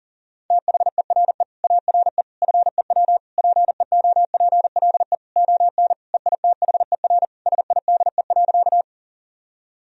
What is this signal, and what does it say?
Morse code, heard as a keyed tone, about 40 wpm, 700 Hz. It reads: THERE ARE FEW PEOPLE ON EITHER SIDE.
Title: Sentences from Top 1000 Words - Review Part 4 - 40wpm